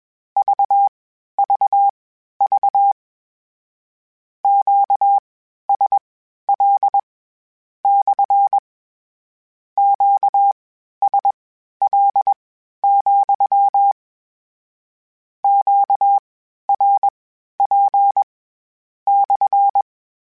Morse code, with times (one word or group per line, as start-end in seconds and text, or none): 0.4-2.9 VVV
4.4-7.0 QSL
7.8-8.6 /
9.8-13.9 QSL,
15.4-18.2 QRP
19.1-19.8 /